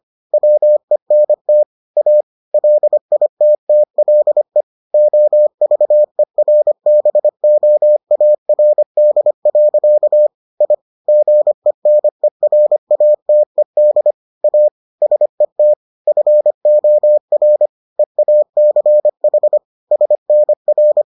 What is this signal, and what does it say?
Morse code, heard as a keyed tone, about 25 wpm, 600 Hz. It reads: WENT A LITTLE OVERBOARD. I GENERATED A SET FOR EACH SNR